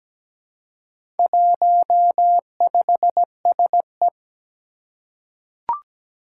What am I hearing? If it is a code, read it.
Morse code, heard as a keyed tone, about 17 wpm, 700 Hz. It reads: 15SE
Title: Four Challenging Letter-Number Combinations 17wpm